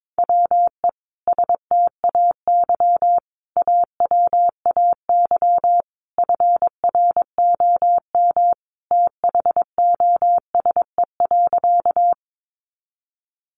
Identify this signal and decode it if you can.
Morse code, heard as a keyed tone, about 22 wpm, 700 Hz. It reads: WE STAY AWAY FROM THOSE.
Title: A Sound Of Thunder 22wpm